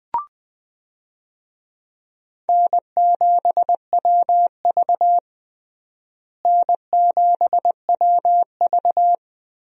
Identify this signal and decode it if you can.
Morse code, heard as a keyed tone, about 20 wpm, 700 Hz. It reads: N7WV N7WV
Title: USA Call Signs - 4 Characters - Repeated 2x - 20wpm